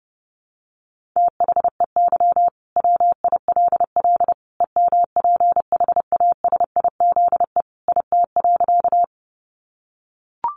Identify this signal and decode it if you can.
Morse code, heard as a keyed tone, about 30 wpm, 700 Hz. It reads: THEY WILL EMPHASIZE IT.
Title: Sentences from Top 1500 Words 30wpm